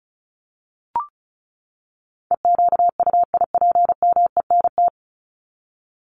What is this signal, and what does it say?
Morse code, heard as a keyed tone, about 34 wpm, 700 Hz. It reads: EQUIPMENT
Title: Top 501-750 Words 35wpm